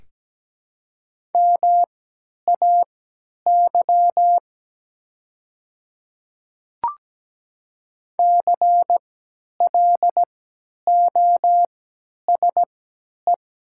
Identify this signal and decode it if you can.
Morse code, heard as a keyed tone, about 17 wpm, spaced as for 10 wpm, 700 Hz. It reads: MAY CLOSE
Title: 3x Character Spacing - Top 300 Words - 17wpm